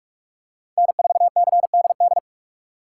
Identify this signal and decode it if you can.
Morse code, heard as a keyed tone, about 44 wpm, 700 Hz. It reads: N4CDD